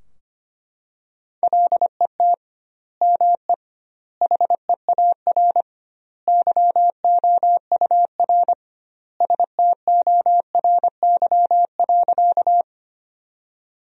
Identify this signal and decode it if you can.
Morse code, heard as a keyed tone, about 25 wpm, 700 Hz. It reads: LET ME HEAR YOUR STORY.
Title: 2x Word-Spacing - Sentences from Top 300 Words - 25wpm